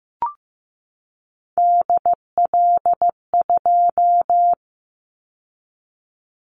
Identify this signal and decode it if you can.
Morse code, heard as a keyed tone, about 15 wpm, 700 Hz. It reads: DL2
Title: Callsign Prefixes for Top 25 DXCC - 15wpm